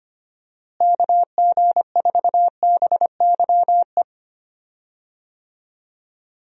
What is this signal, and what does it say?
Morse code, heard as a keyed tone, about 25 wpm, 700 Hz. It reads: KG4BYE
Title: USA Call Signs - 6 Characters - Repeated 3x - 25wpm